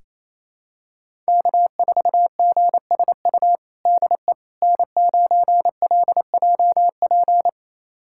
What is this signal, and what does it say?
Morse code, heard as a keyed tone, about 28 wpm, 700 Hz. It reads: K4GSU DE N9LJP
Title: USA Callsign DE Callsign 28wpm